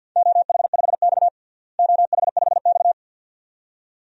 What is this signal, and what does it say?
Morse code, heard as a keyed tone, about 52 wpm, 700 Hz. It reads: OH5X OH5X
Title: DXCC-100 Call Signs - 4 Characters - Repeated 2x - 50wpm